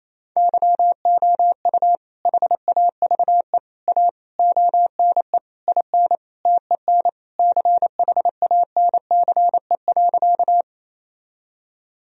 Morse code, heard as a keyed tone, about 28 wpm, 700 Hz. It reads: YOU HAVE A ONE IN TEN CHANCE.
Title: Sentences from Top 1000 Words - Review Part 2 - 28wpm